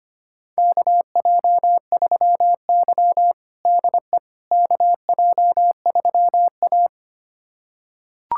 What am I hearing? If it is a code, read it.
Morse code, heard as a keyed tone, about 25 wpm, 700 Hz. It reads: KJ3Y DE KJ3A
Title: USA Callsign DE Callsign - 4 Characters - 25wpm